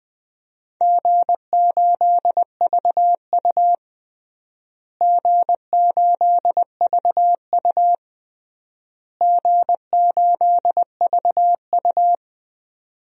Morse code, heard as a keyed tone, about 20 wpm, 700 Hz. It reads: G8VU G8VU G8VU
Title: DXCC-100 Call Signs - 4 Characters - Repeated 3x - 20wpm